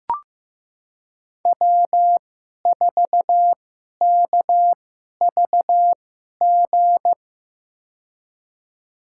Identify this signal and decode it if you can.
Morse code, heard as a keyed tone, about 15 wpm, spaced as for 11 wpm, 700 Hz. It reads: W4KVG
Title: USA Call Signs - 2x Character Spacing - 15wpm